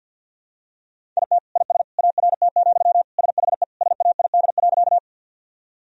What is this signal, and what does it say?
Morse code, heard as a keyed tone, about 48 wpm, 700 Hz. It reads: IT IS ART, SHE SAID.
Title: Sentences from Top 1000 Words - Review Part 3 - 50wpm